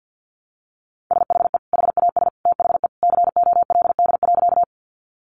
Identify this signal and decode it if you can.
Morse code, heard as a keyed tone, about 50 wpm, 700 Hz. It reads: SHE HAS THE COLD.
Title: Easy Phrases 50wpm